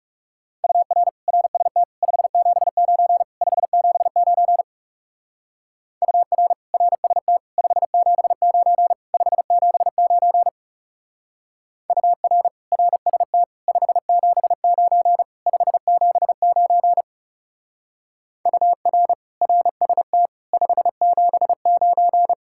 Morse code, heard as a keyed tone, about 35 wpm, 700 Hz. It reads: UR RST 579 579 UR RST 579 579 UR RST 579 579 UR RST 579
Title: Speed-Racing Common QSO Phrases - 30wpm